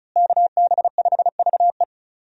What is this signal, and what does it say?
Morse code, heard as a keyed tone, about 36 wpm, 700 Hz. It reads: KB5VE
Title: USA Call Signs - 5 Characters - Repeated 3x - 35wpm